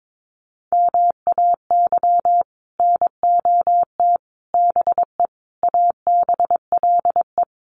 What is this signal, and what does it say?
Morse code, heard as a keyed tone, about 22 wpm, 700 Hz. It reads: MAY NOT BE ABLE
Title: Sets of 5 Words 22wpm